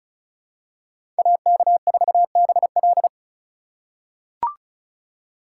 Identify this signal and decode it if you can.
Morse code, heard as a keyed tone, about 36 wpm, 700 Hz. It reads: AK4BL
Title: USA Call Signs - 5 Characters - 3x Character Spacing - 35wpm